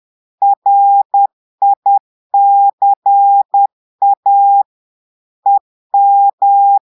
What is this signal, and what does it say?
Morse code, heard as a keyed tone, about 10 wpm, 800 Hz. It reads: RICA EM
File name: Portugais - texte en clair 1 - vitesse - 10 mots minute